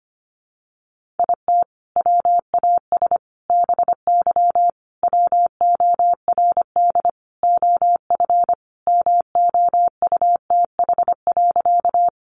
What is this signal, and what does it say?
Morse code, heard as a keyed tone, about 25 wpm, 700 Hz. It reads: IT WAS BY WORD OF MOUTH.
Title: Sentences from Top 1500 Words 25wpm